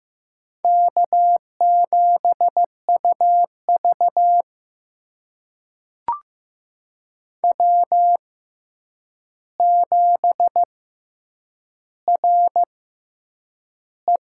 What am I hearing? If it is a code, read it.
Morse code, heard as a keyed tone, about 15 wpm, spaced as for 8 wpm, 700 Hz. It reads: K7UV W7RE
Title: USA Call Signs - 4 Characters - 6x Character Spacing - 15wpm